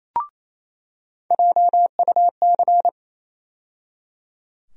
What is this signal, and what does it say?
Morse code, heard as a keyed tone, about 28 wpm, 700 Hz. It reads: JUC
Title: USA Call Sign Postfixes - 3 Letters 28wpm